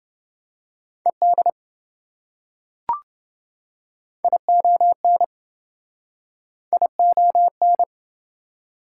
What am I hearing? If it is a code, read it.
Morse code, heard as a keyed tone, about 25 wpm, 700 Hz. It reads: ED ION ION
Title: Speed-Racing Prefixes and Suffixes 20wpm